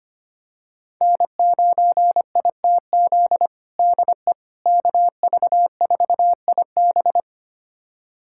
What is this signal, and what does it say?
Morse code, heard as a keyed tone, about 25 wpm, 700 Hz. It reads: N9ITZ DE KV4IB
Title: USA Callsign DE Callsign - 5 Characters - 25wpm